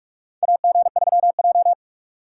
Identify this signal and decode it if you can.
Morse code, heard as a keyed tone, about 46 wpm, 700 Hz. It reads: AG3J
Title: USA Call Signs - 4 Characters - 2x Character Spacing - 45wpm